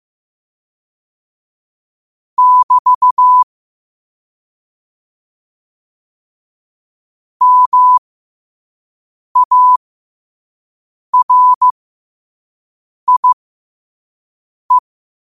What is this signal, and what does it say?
Morse code, heard as a keyed tone, about 15 wpm, spaced as for 5 wpm, 1000 Hz. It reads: <BT> MARIE
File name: qotdfiles / 5wpm